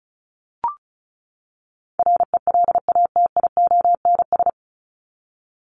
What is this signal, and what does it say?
Morse code, heard as a keyed tone, about 34 wpm, 700 Hz. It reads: RELATIONS
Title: ES:S Suffix - Top 100 Words with Base Words 35wpm